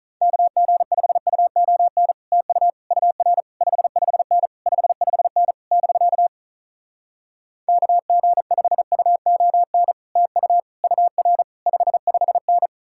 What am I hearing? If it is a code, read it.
Morse code, heard as a keyed tone, about 47 wpm, 700 Hz. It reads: KG5UON TU UR 55N 55N <BK> KG5UON TU UR 55N